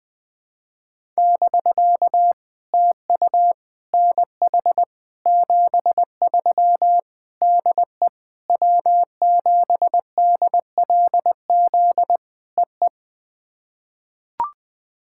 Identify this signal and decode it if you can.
Morse code, heard as a keyed tone, about 20 wpm, 700 Hz. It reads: <BK> TU NH 73 DE W7DLZ EE
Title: POTA QSO Practice - 20wpm